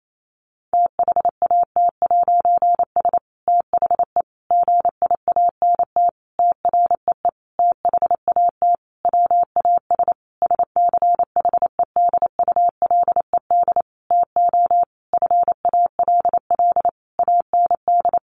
Morse code, heard as a keyed tone, about 28 wpm, 700 Hz. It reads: THAT'S THE GIANT TREE THAT WAS SCHEDULED TO FALL AND
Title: A Sound Of Thunder 28wpm